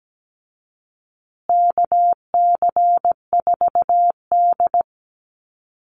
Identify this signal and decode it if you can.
Morse code, heard as a keyed tone, about 17 wpm, 700 Hz. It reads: KC4D
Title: Call Signs Repeated 3x - 17wpm